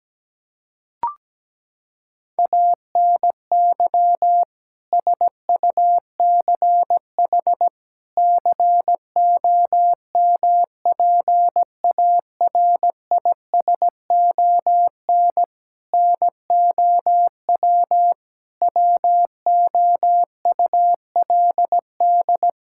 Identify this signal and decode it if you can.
Morse code, heard as a keyed tone, about 17 wpm, 700 Hz. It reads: ANY SUCH COMPARISON NOW WOULD
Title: Sentences from Top 1600 Words 17wpm